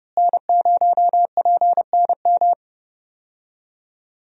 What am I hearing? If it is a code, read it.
Morse code, heard as a keyed tone, about 30 wpm, 700 Hz. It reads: N0PNM
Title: USA Call Signs - 5 Characters - Repeated 2x - 30wpm